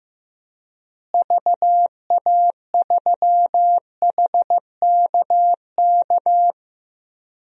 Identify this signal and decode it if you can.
Morse code, heard as a keyed tone, about 15 wpm, 700 Hz. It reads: VA3HKK
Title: DXCC-100 Call Signs - 6 Characters - Repeated 2x - 15wpm